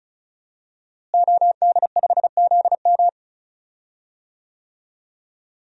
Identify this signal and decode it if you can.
Morse code, heard as a keyed tone, about 34 wpm, 700 Hz. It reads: OD5ZM
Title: DXCC-100 Call Signs - 5 Characters - Repeated 3x - 35wpm